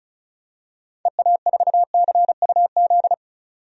Morse code, heard as a keyed tone, about 34 wpm, 700 Hz. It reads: EA4CUZ
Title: DXCC-100 Call Signs - 6 Characters - 35wpm